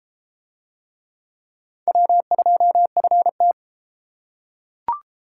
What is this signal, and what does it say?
Morse code, heard as a keyed tone, about 31 wpm, 700 Hz. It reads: W2FT